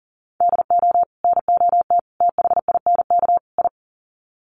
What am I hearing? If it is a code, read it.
Morse code, heard as a keyed tone, about 37 wpm, 700 Hz. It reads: DO NOT THINK I
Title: Sets of 4 Words 40wpm